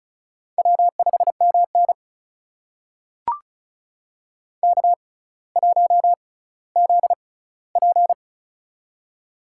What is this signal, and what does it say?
Morse code, heard as a keyed tone, about 36 wpm, spaced as for 18 wpm, 700 Hz. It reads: W5MN K1ZP
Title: USA Call Signs - 4 Characters - 6x Character Spacing - 35wpm